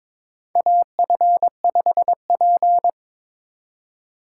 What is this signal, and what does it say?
Morse code, heard as a keyed tone, about 22 wpm, 700 Hz. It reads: AF5P
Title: K1USN SST QSO Practice - 22wpm